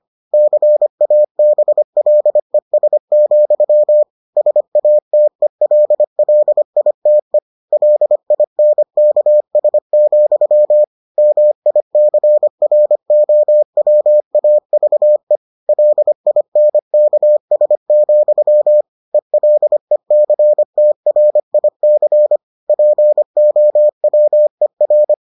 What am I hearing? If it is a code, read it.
Morse code, heard as a keyed tone, about 25 wpm, 600 Hz. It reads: CABLES, SATELLITE LINKS, MICROWAVE LINKS, ELECTRIC POWER